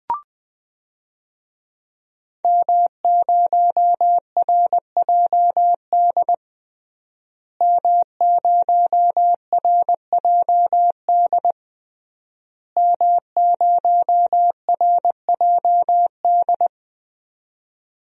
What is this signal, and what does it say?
Morse code, heard as a keyed tone, about 20 wpm, 700 Hz. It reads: M0RJD M0RJD M0RJD
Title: DXCC-100 Call Signs - 5 Characters - Repeated 3x - 20wpm